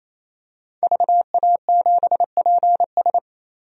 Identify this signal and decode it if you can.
Morse code, heard as a keyed tone, about 27 wpm, 700 Hz. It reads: VA7PS
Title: DXCC-100 Call Signs - 5 Characters - 28wpm